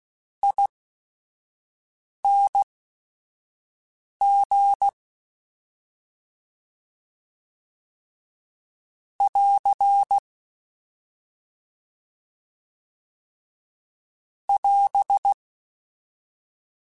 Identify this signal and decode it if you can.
Morse code, heard as a keyed tone, about 16 wpm, spaced as for 5 wpm, 800 Hz. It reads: ING <AR> <AS>